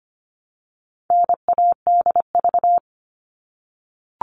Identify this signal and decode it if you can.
Morse code, heard as a keyed tone, about 25 wpm, 700 Hz. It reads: NADV
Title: All Cut Numbers - 4-Digit 25wpm